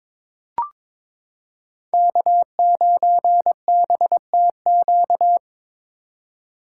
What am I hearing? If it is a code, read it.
Morse code, heard as a keyed tone, about 22 wpm, 700 Hz. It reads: K9BTQ
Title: K1USN SST QSO Practice - 22wpm